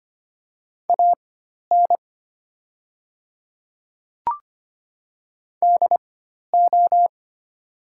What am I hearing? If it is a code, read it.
Morse code, heard as a keyed tone, about 25 wpm, spaced as for 12 wpm, 700 Hz. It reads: AN DO
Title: 4x Character Spacing - Top 200 Words - 25wpm